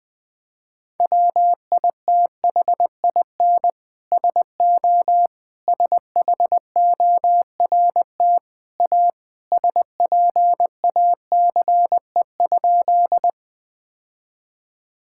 Morse code, heard as a keyed tone, about 20 wpm, 700 Hz. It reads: WITHIN SO SHORT A SPACE?
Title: Sentences from Top 1000 Words - Review Part 5 - 20wpm